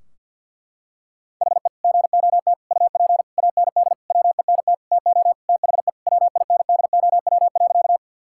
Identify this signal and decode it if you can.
Morse code, heard as a keyed tone, about 50 wpm, 700 Hz. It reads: HE GOT UP AND WENT TO THE WINDOW.
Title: Sentences from Top 1600 Words 50wpm